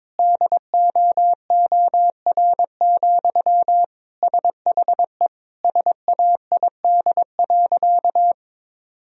Morse code, heard as a keyed tone, about 21 wpm, 700 Hz. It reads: DOOR, SHE SAID.
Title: Sentences from Top 1000 Words - Review Part 1 - 22wpm